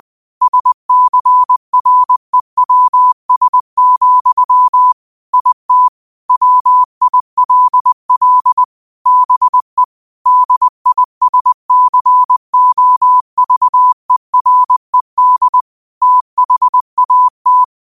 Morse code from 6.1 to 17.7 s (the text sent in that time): WILL BE DISCOVERED THAT